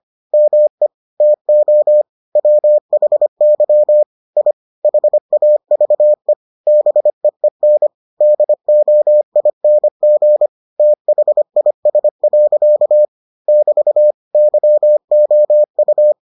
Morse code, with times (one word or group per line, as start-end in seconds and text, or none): 0.3-0.9 ME
1.2-2.0 TO
2.4-4.0 WHY
4.4-4.5 I
4.8-6.3 HAVE
6.7-7.9 BEEN
8.2-10.5 DOING
10.8-13.1 THIS.
13.5-14.1 <BT>
14.3-16.1 YOU